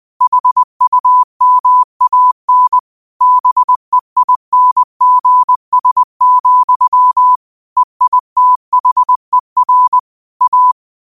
Morse code, heard as a keyed tone, about 20 wpm, 1000 Hz. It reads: HUMAN BEINGS, EITHER A
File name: qotdfiles / 20wpm